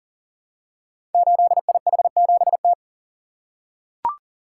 Morse code, heard as a keyed tone, about 39 wpm, 690 Hz. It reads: 8IH7T
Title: Code Groups - 5 Letter:Numbers - 40wpm